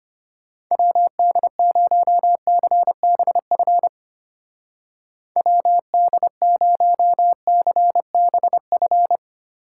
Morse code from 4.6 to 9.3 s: WD0CBF

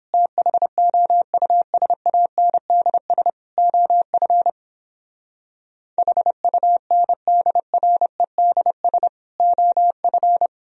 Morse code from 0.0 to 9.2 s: THOUSANDS OF HUNDREDS

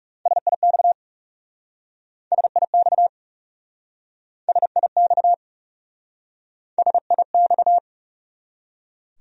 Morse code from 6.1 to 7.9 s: SIX

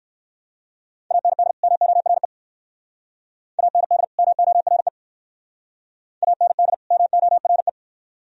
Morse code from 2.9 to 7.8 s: AND MORE AND MORE

